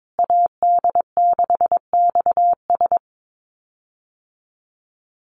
Morse code, heard as a keyed tone, about 22 wpm, 700 Hz. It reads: AD6XS
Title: USA Call Signs Repeated 2x 22wpm